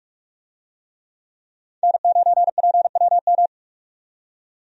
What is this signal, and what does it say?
Morse code, heard as a keyed tone, about 43 wpm, 700 Hz. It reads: N9PWM
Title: USA Call Signs - 5 Characters - 45wpm